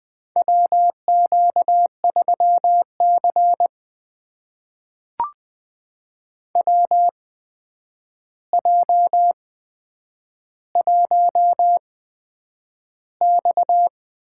Morse code, heard as a keyed tone, about 20 wpm, spaced as for 9 wpm, 700 Hz. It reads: WQ3C WJ1X
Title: USA Call Signs - 4 Characters - 8x Character Spacing - 20wpm